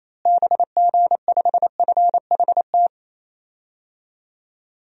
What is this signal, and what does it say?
Morse code, heard as a keyed tone, about 28 wpm, 700 Hz. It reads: BG5FHT
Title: DXCC-100 Call Signs - 6 Characters - Repeated 3x - 28wpm